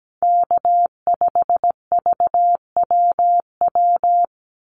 Morse code, heard as a keyed tone, about 17 wpm, 700 Hz. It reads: K5VWW
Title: Call Signs - 17wpm